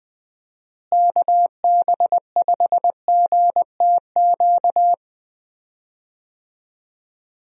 Morse code, heard as a keyed tone, about 20 wpm, 700 Hz. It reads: KB5GTQ
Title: USA Call Signs - 6 Characters - Repeated 2x - 20wpm